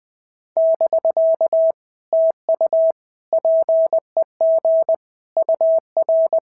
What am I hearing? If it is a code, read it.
Morse code, heard as a keyed tone, about 20 wpm, 660 Hz. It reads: <BK> TU PEG UR
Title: POTA QSO Practice - 20wpm